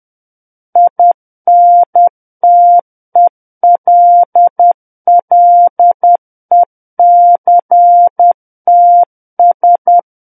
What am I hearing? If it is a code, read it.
Morse code, heard as a keyed tone, about 10 wpm, 700 Hz. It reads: INTELLECTS
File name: Book One / 01-Chapter 1 - 10 WPM